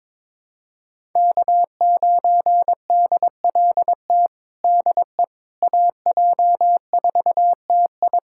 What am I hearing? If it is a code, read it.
Morse code, heard as a keyed tone, about 22 wpm, 700 Hz. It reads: K9DLT DE AJ4TI
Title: USA Callsign DE Callsign - 5 Characters - 22wpm